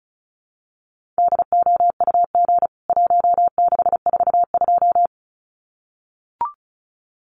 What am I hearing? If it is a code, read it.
Morse code, heard as a keyed tone, about 34 wpm, 700 Hz. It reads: DOUG 1642
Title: CWT Exchange 35wpm